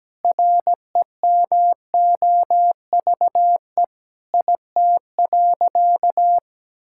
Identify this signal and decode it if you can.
Morse code, heard as a keyed tone, about 17 wpm, 700 Hz. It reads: REMOVE IT.
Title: Sentences from Top 1000 Words - Review Part 5 - 17wpm